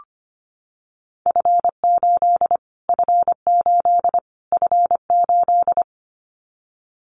Morse code, heard as a keyed tone, about 25 wpm, 700 Hz. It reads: F8 F8 F8
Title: Callsign Prefixes for Top 25 DXCC - Repeated 3x - 25wpm